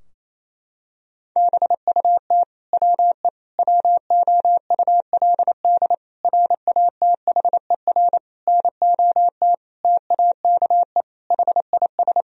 Morse code, heard as a keyed tone, about 27 wpm, 700 Hz. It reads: BUT WE WOULD RATHER NOT TAKE HIS
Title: Sentences from Top 1000 Words - Review Part 4 - 28wpm